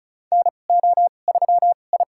Morse code, heard as a keyed tone, about 34 wpm, spaced as for 25 wpm, 700 Hz. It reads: NO3I
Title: USA Call Signs - 2x Character Spacing - 35wpm